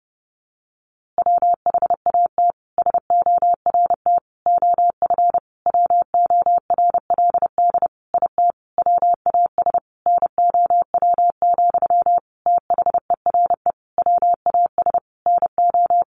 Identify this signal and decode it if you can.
Morse code, heard as a keyed tone, about 30 wpm, 700 Hz. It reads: WHAT SORT OF WORLD IT WAS NOW, THERE WAS NO
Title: A Sound Of Thunder 30wpm